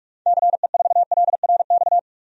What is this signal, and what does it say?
Morse code, heard as a keyed tone, about 45 wpm, 700 Hz. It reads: CE4LRX
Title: DXCC-100 Call Signs - 6 Characters - 45wpm